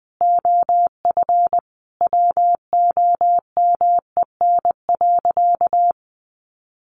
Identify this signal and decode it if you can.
Morse code, heard as a keyed tone, about 20 wpm, 700 Hz. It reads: OF WOMEN.
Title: Sentences from Top 1400 Words 20wpm